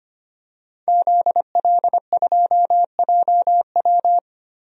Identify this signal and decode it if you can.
Morse code, heard as a keyed tone, about 25 wpm, 700 Hz. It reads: ZL2JW
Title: DXCC-100 Call Signs - 5 Characters - 25wpm